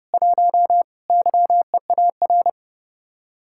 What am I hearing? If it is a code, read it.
Morse code, heard as a keyed tone, about 34 wpm, 700 Hz. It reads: 1 YEAR
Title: Speed-Racing Common QSO Phrases - 30wpm